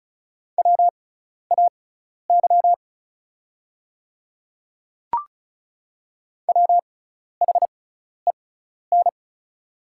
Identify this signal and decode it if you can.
Morse code, heard as a keyed tone, about 36 wpm, spaced as for 12 wpm, 700 Hz. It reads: WAY WHEN
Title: 6x Character Spacing - Top 100 Words - 35wpm